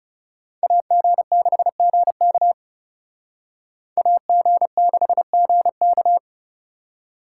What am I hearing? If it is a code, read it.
Morse code, heard as a keyed tone, about 34 wpm, 700 Hz. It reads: AG6GK AG6GK
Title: Speed Racing - USA 2x2 Call Signs 30wpm